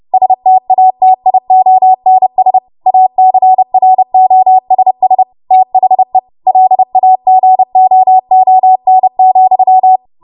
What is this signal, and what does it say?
Morse code, heard as a keyed tone, about 30 wpm, 750 Hz. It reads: STATIONS ACROSS THE LAGOON,
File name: Morse / 30WPM